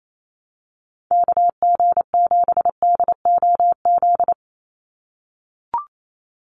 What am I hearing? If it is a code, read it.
Morse code, heard as a keyed tone, about 28 wpm, 700 Hz. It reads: KG7DOZ
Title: USA Call Signs 28wpm